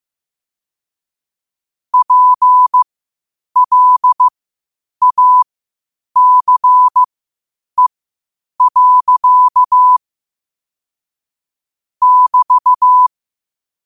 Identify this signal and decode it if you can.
Morse code, heard as a keyed tone, about 15 wpm, spaced as for 8 wpm, 1000 Hz. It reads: PLACE. <BT>
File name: qotdfiles / 8wpm